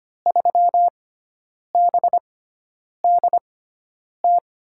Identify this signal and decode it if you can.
Morse code, heard as a keyed tone, about 25 wpm, spaced as for 9 wpm, 700 Hz. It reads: 3BDT
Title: USA Call Signs - 6x Character Spacing - 25wpm